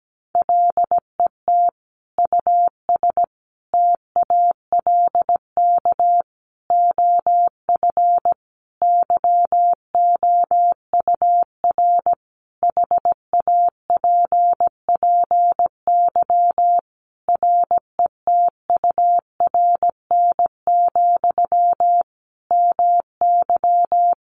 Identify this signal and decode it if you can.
Morse code, heard as a keyed tone, about 17 wpm, 700 Hz. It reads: LET US TALK OF YOUR HAPPY RETURN, MY
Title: Sentences from Top 1000 Words 17wpm